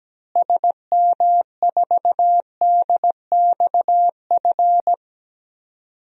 Morse code, heard as a keyed tone, about 17 wpm, 700 Hz. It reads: SM4DXF
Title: DXCC-100 Call Signs - 6 Characters - Repeated 2x - 17wpm